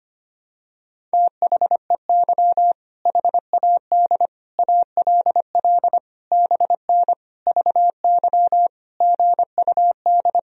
Morse code, heard as a keyed tone, about 25 wpm, 700 Hz. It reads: THEY HAD ALL BN VY GUD
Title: Sentences from Top 200 Words and Abbreviations 25wpm